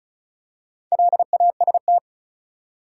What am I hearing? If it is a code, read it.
Morse code, heard as a keyed tone, about 36 wpm, 700 Hz. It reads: LAST
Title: Four Letter Words - 35wpm